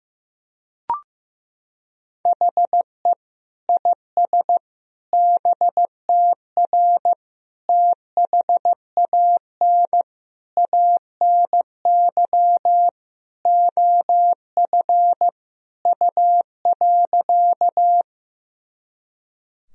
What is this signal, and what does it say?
Morse code, heard as a keyed tone, about 15 wpm, 700 Hz. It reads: HE IS BTR THAN ANY OF U.
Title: Sentences from Top 400 Words and Abbreviations 15wpm